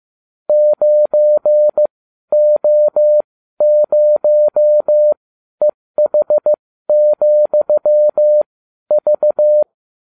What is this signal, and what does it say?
Morse code, heard as a keyed tone, about 15 wpm, spaced as for 12 wpm, 600 Hz. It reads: 9O0EH,V